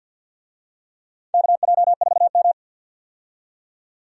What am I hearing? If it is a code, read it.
Morse code, heard as a keyed tone, about 48 wpm, 700 Hz. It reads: KJ4M
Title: USA Call Signs - 4 Characters - 50wpm